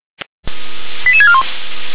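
Five-tone selective call (ZVEI), heard as quick runs of five tones, tones of about 70 ms. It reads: 88641
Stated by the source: Five-tone ZVEI selective call 88912 received on an SDR.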